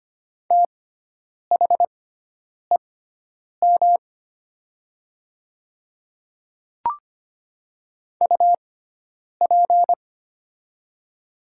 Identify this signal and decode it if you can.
Morse code, heard as a keyed tone, about 25 wpm, spaced as for 9 wpm, 700 Hz. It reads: THEM UP
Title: 6x Character Spacing - Top 100 Words - 25wpm